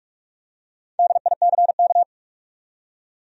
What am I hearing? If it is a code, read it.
Morse code, heard as a keyed tone, about 46 wpm, 700 Hz. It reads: DICK
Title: USA Common Names - Max 4 Letters - 45wpm